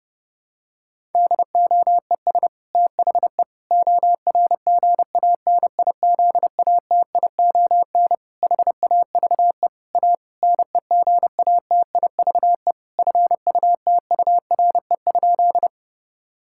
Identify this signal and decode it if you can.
Morse code, heard as a keyed tone, about 30 wpm, 700 Hz. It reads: DOES THE ORGANIZATION HAVE A NEGATIVE FUTURE?
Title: Sentences from Top 1000 Words - Review Part 6 - 30wpm